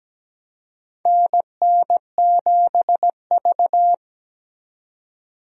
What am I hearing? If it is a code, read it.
Morse code, heard as a keyed tone, about 17 wpm, 700 Hz. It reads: NN7V
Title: USA Call Signs - 4 Characters - 17wpm